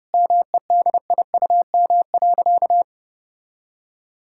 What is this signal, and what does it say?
Morse code, heard as a keyed tone, about 30 wpm, 700 Hz. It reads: MEDIUM.
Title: Sentences from Top 1000 Words - Review Part 6 - 30wpm